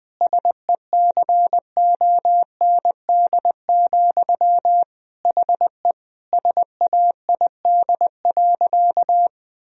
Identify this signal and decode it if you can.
Morse code, heard as a keyed tone, about 21 wpm, 700 Hz. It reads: SECOND, HE SAID.
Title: Sentences from Top 1000 Words - Review Part 1 - 20wpm